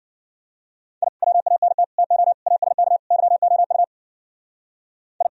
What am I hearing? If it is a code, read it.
Morse code, heard as a keyed tone, about 52 wpm, 700 Hz. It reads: I WANT TO ASK YOU I